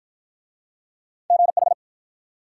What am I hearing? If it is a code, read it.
Morse code, heard as a keyed tone, about 54 wpm, 700 Hz. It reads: GH